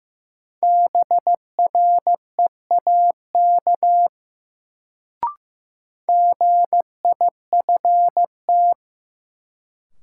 Morse code, heard as a keyed tone, about 15 wpm, 700 Hz. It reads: BREAK GIFT
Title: Sentences from Top 900 Words - New Word Drill - 15wpm